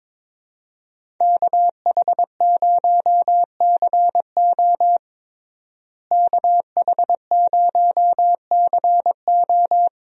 Morse code, heard as a keyed tone, about 22 wpm, 700 Hz. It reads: KH0CO KH0CO
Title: USA Call Signs - 5 Characters - Repeated 2x - 22wpm